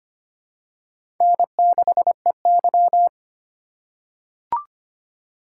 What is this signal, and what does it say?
Morse code, heard as a keyed tone, about 25 wpm, 700 Hz. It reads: N6EY
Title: USA Call Signs - 4x Character Spacing - 25wpm